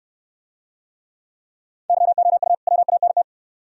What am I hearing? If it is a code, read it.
Morse code, heard as a keyed tone, about 65 wpm, 700 Hz. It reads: YOU WANT